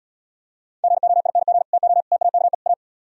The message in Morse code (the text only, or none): GOING TO TAKE A